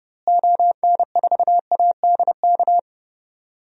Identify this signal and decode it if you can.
Morse code, heard as a keyed tone, about 30 wpm, 700 Hz. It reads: ON4ADK
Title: DXCC-100 Call Signs - 6 Characters - Repeated 3x - 30wpm